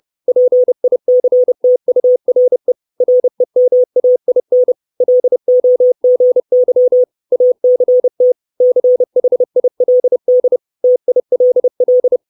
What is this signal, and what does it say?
Morse code, heard as a keyed tone, about 30 wpm, 500 Hz. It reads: PICTURE REMAIN LOGY ACT CHILD TILL
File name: 632 Words 30WPM